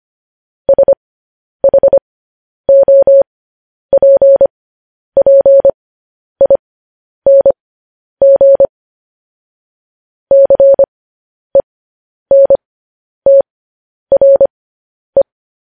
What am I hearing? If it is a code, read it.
Morse code, heard as a keyed tone, about 25 wpm, spaced as for 10 wpm, 550 Hz. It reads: SHOPPING CENTRE